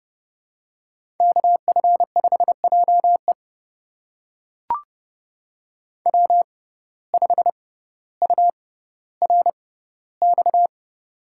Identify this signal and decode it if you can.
Morse code, heard as a keyed tone, about 30 wpm, spaced as for 15 wpm, 700 Hz. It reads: KF5JE W5URX
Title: USA Call Signs - 5 Characters - 6x Character Spacing - 30wpm